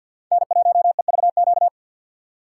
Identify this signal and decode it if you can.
Morse code, heard as a keyed tone, about 62 wpm, 700 Hz. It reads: N1EVQ
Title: Speed Racing - USA 1x3 Call Signs 50wpm